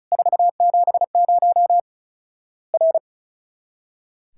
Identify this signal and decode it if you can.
Morse code, heard as a keyed tone, about 34 wpm, 700 Hz. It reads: 470 R
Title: Worked All Europe DX Contest - QTC Exchange - 35wpm